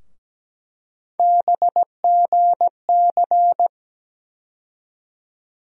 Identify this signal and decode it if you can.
Morse code, heard as a keyed tone, about 17 wpm, 700 Hz. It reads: BGC
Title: USA Call Sign Postfixes - 3 Letters 17wpm